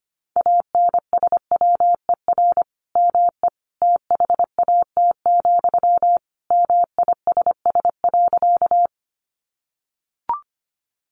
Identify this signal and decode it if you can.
Morse code, heard as a keyed tone, about 25 wpm, 700 Hz. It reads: ANSWER ME THAT, MISS.
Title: Sentences from Top 500 Words 25wpm